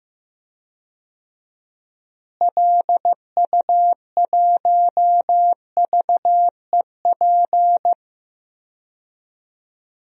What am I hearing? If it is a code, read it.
Morse code, heard as a keyed tone, about 15 wpm, 700 Hz. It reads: LU1VEP
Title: DXCC-100 Call Signs - 6 Characters - Repeated 2x - 15wpm